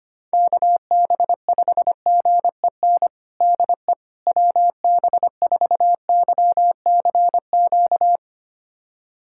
Morse code, heard as a keyed tone, about 25 wpm, 700 Hz. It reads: KB5GEN DE WB4YCQ
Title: USA Callsign DE Callsign - 6 Characters - 25wpm